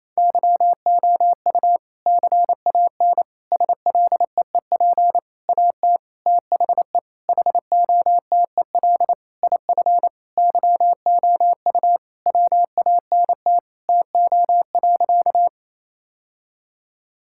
Morse code, heard as a keyed tone, about 28 wpm, 700 Hz. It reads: YOU CAN SLEEP AT THE HOTEL IF YOU WANT TO.
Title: Sentences from Top 1600 Words 28wpm